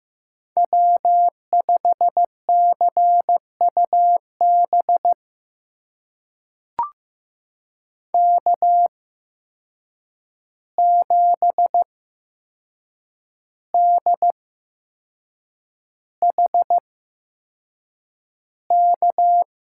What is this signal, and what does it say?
Morse code, heard as a keyed tone, about 15 wpm, spaced as for 6 wpm, 700 Hz. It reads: W5CUB K7DHK
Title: USA Call Signs - 5 Characters - 8x Character Spacing - 15wpm